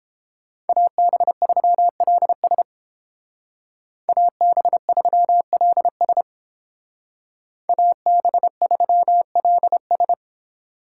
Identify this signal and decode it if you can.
Morse code, heard as a keyed tone, about 29 wpm, 700 Hz. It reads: AB3LS AB3LS AB3LS